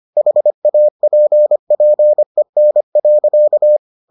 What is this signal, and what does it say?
Morse code, heard as a keyed tone, about 25 wpm, 600 Hz. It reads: HAPPEN.